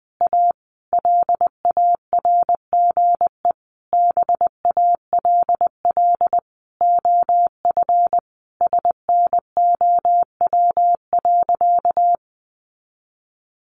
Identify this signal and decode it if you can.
Morse code, heard as a keyed tone, about 20 wpm, 700 Hz. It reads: A LARGE BALL OF SNOW.
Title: Easy Phrases 20wpm